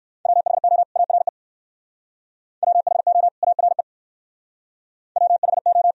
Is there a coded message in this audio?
Yes: Morse code from 0.3 to 5.9 s, heard as a keyed tone, about 64 wpm, 700 Hz. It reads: WHO ARE WHO ARE WHO